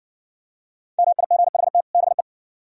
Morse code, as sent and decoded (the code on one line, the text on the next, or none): -- .. --. .... - / -... .
MIGHT BE